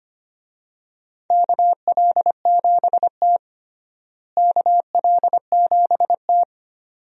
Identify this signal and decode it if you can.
Morse code, heard as a keyed tone, about 25 wpm, 700 Hz. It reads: KL7T KL7T
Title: DXCC-100 Call Signs - 4 Characters - Repeated 2x - 25wpm